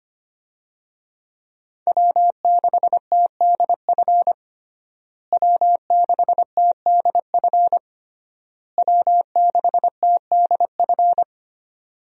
Morse code, heard as a keyed tone, about 25 wpm, 700 Hz. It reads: W6TDF W6TDF W6TDF
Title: USA Call Signs - 5 Characters - Repeated 3x - 25wpm